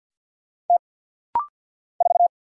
Morse code, heard as a keyed tone, about 50 wpm, 690 Hz. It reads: T 4
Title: Single Letter-Number - Rapid-Fire 50wpm